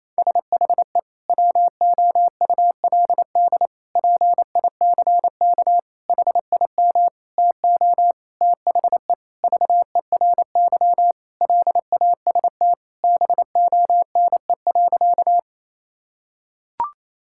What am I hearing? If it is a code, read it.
Morse code, heard as a keyed tone, about 28 wpm, 700 Hz. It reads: SHE WOULD PICK HIM TO THE VERY LAST BONE.
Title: Sentences from Top 1600 Words 28wpm